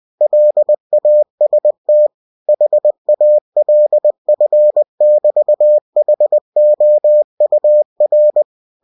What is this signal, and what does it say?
Morse code, heard as a keyed tone, about 20 wpm, 600 Hz. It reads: LAST HALF<BT>HOUR
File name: cw-books / winnie-the-pooh / winnie-20wpm-page-0100